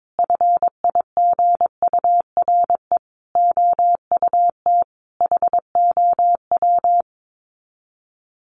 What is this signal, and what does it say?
Morse code, heard as a keyed tone, about 22 wpm, 700 Hz. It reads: FIGURE OUT HOW
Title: Sets of 5 Words Encore 22wpm